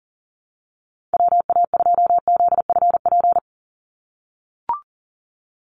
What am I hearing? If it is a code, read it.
Morse code, heard as a keyed tone, about 40 wpm, 700 Hz. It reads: WA2ZFP
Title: USA Call Signs 40wpm